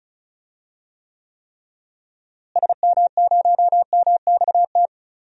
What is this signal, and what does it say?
Morse code, heard as a keyed tone, about 36 wpm, 700 Hz. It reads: SM0MXT
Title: DXCC-100 Call Signs - 6 Characters - Repeated 3x - 35wpm